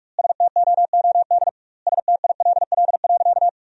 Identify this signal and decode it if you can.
Morse code, heard as a keyed tone, about 46 wpm, 700 Hz. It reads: STOOD STILL.
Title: 2x Word-Spacing - Sentences from Top 500 Words - 45wpm